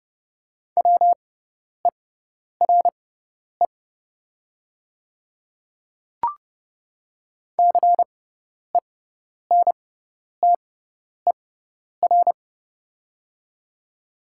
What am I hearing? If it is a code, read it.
Morse code, heard as a keyed tone, about 30 wpm, spaced as for 10 wpm, 690 Hz. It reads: WERE CENTER
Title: 6x Character Spacing - Top 500 Words - 30wpm